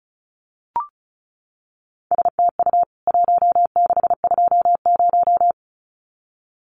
Morse code, heard as a keyed tone, about 34 wpm, 700 Hz. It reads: STU 1620
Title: CWT Exchange 35wpm